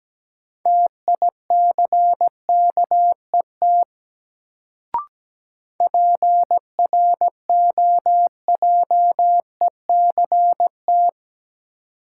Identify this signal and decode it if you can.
Morse code, heard as a keyed tone, about 17 wpm, 700 Hz. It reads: TICKET PROJECT
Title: Sentences from Top 1400 Words - New Word Drill - 17wpm